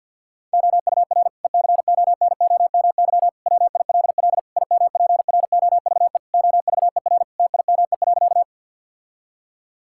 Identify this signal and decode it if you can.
Morse code, heard as a keyed tone, about 50 wpm, 700 Hz. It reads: OUR ECONOMY WILL IMPROVE OVER TIME.
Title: Sentences from Top 1000 Words - Review Part 5 - 50wpm